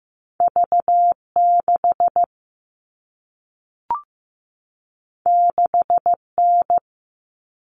Morse code, heard as a keyed tone, about 15 wpm, 700 Hz. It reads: V6 6N
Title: All Cut Numbers - 2-Digit 15wpm